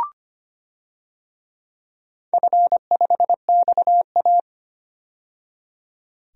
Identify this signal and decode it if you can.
Morse code, heard as a keyed tone, about 25 wpm, 700 Hz. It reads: F5XA
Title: DXCC-100 Call Signs - 4 Characters - 25wpm